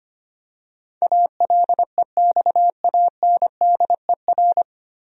Morse code, heard as a keyed tone, about 25 wpm, 700 Hz. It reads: ALEXANDER
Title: USA Common Names - 25wpm